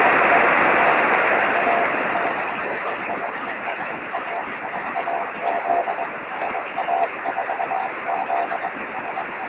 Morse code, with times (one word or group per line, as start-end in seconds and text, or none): none